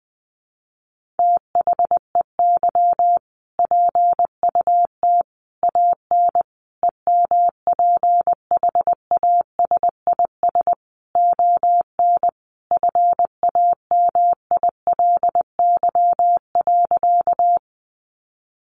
Morse code, heard as a keyed tone, about 20 wpm, 700 Hz. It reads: THEY PUT AN EMPHASIS ON FAMILY.
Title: Sentences from Top 1400 Words 20wpm